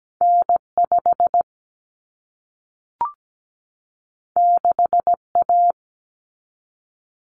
Morse code, heard as a keyed tone, about 17 wpm, 700 Hz. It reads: N5 6A
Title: Common Cut Numbers - 2-Digit 17wpm